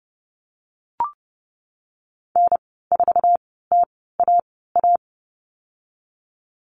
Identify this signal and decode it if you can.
Morse code, heard as a keyed tone, about 30 wpm, spaced as for 17 wpm, 700 Hz. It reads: N4TAA
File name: Lesson 14g - Number 2 - Call Signs 30wpm